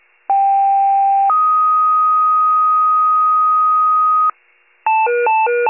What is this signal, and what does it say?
Two-tone page, heard as a one-second tone, then a three-second tone, about 790 then 1250 Hz. It is heard from 0.3 to 4.3 s.